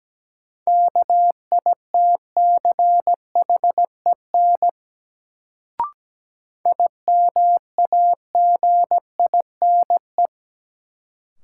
Morse code, heard as a keyed tone, about 17 wpm, 700 Hz. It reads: KITCHEN IMAGINE
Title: Sentences from Top 1400 Words - New Word Drill Alt - 17wpm